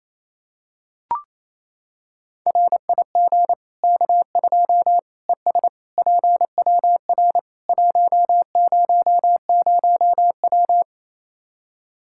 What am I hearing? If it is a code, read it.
Morse code, heard as a keyed tone, about 27 wpm, 700 Hz. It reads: RIG K2 ES PWR 100W
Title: K1USN SST QSO Practice 28wpm